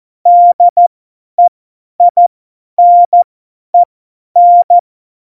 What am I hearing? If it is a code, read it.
Morse code, heard as a keyed tone, about 14 wpm, spaced as for 10 wpm, 700 Hz. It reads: DEINEN
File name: Morsekurs / Witze / Witz 15. Dezember 2020, 50 F70